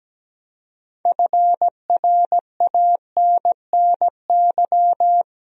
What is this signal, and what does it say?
Morse code, heard as a keyed tone, about 17 wpm, 700 Hz. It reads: FRANNY
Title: USA Common Names - 17wpm